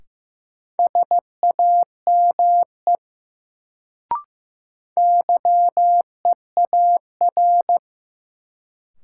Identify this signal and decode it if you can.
Morse code, heard as a keyed tone, about 16 wpm, 700 Hz. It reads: SAME YEAR
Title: Sets of 2 Words - New Word Drill - 15wpm